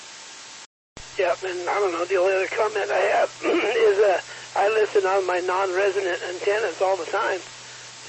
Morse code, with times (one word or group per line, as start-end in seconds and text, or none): none